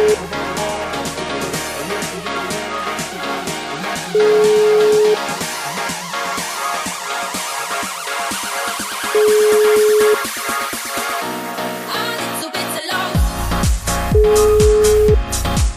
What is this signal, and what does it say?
Ringback tone (Europe, 425 Hz), with noise.